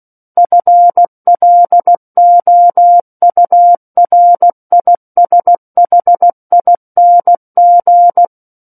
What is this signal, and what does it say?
Morse code, heard as a keyed tone, about 16 wpm, 700 Hz. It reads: FLOURISHING
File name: Book One / 07-Chapter 7 - 16 WPM